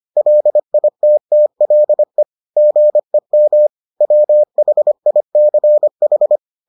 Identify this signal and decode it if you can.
Morse code, heard as a keyed tone, about 25 wpm, 600 Hz. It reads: LITTLE GEM WHICH